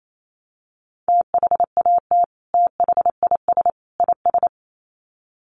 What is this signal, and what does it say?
Morse code, heard as a keyed tone, about 28 wpm, 700 Hz. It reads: THAT THIS IS
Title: Sets of 3 Words 28wpm